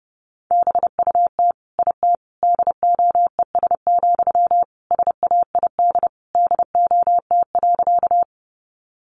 Morse code, heard as a keyed tone, about 30 wpm, 700 Hz. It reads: BUT IT DOES, SAID DOT.
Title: Sentences from Top 1500 Words 30wpm